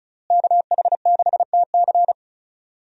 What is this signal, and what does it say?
Morse code, heard as a keyed tone, about 36 wpm, 700 Hz. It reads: KH6TC
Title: DXCC-100 Call Signs - 5 Characters - Repeated 2x - 35wpm